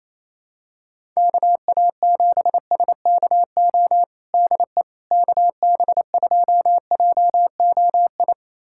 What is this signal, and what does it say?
Morse code, heard as a keyed tone, about 27 wpm, 700 Hz. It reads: KA7SKO DE KB2JOI
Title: USA Callsign DE Callsign - 6 Characters - 28wpm